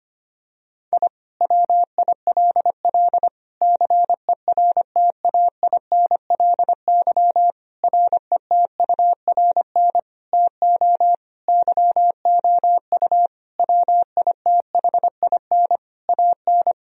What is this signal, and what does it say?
Morse code, heard as a keyed tone, about 25 wpm, 700 Hz. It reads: I WILL CERTAINLY RETURN TO YOU WITHIN AN
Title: Sentences from Top 1000 Words - Review Part 4 - 25wpm